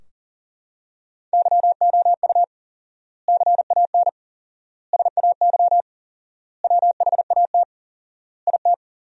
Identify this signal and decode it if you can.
Morse code, heard as a keyed tone, about 40 wpm, 700 Hz. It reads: YOU CAN SAY WHAT IT
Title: 4x Word-Spacing - Sentences from Top 200 Words - 40wpm